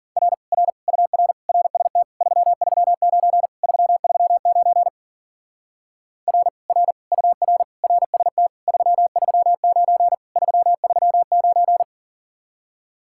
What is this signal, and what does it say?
Morse code, heard as a keyed tone, about 52 wpm, 700 Hz. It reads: R R UR RST 339 339 R R UR RST 339 339